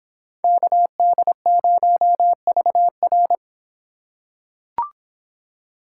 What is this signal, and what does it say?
Morse code, heard as a keyed tone, about 24 wpm, 700 Hz. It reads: KD0VR